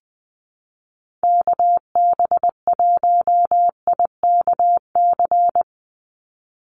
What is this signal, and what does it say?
Morse code, heard as a keyed tone, about 20 wpm, 700 Hz. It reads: KB1IKC
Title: USA Call Signs 20wpm